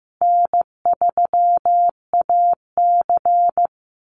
Morse code, heard as a keyed tone, about 15 wpm, 700 Hz. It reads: N3AC
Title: Easy Call Signs Repeated 3x - 15wpm